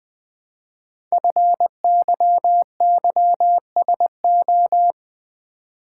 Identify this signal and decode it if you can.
Morse code, heard as a keyed tone, about 20 wpm, 700 Hz. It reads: FYYSO
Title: Code Groups - 5 Letters - 20wpm